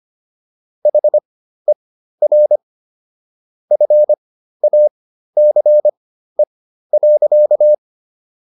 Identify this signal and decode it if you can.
Morse code, heard as a keyed tone, about 25 wpm, spaced as for 13 wpm, 600 Hz. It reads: HER FACE.